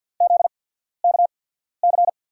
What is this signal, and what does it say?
Morse code, heard as a keyed tone, about 53 wpm, spaced as for 14 wpm, 690 Hz. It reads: ZKC
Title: USA Call Signs - 6 Characters - 8x Character Spacing - 50wpm